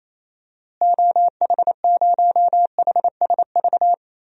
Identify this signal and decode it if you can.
Morse code, heard as a keyed tone, about 28 wpm, 700 Hz. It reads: OH0HSV
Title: DXCC-100 Call Signs - 6 Characters - Repeated 2x - 28wpm